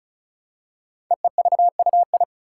Morse code, heard as a keyed tone, about 36 wpm, 690 Hz. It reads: EEVUI
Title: Code Groups - 5 Letters - 35wpm